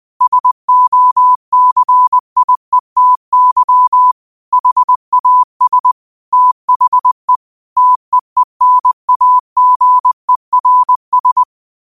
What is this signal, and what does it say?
Morse code, heard as a keyed tone, about 20 wpm, 1000 Hz. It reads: SOCIETY HAS THE TEENAGERS
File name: qotdfiles / 20wpm